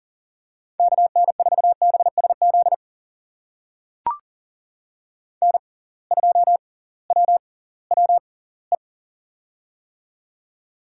Morse code, heard as a keyed tone, about 40 wpm, spaced as for 22 wpm, 700 Hz. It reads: KN4BSZ N2WWE
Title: USA Call Signs - 6x Character Spacing - 40wpm